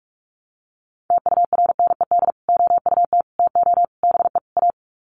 Morse code, heard as a keyed tone, about 44 wpm, 700 Hz. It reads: TURNED OUT TO BE A
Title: Sets of 5 Words 45wpm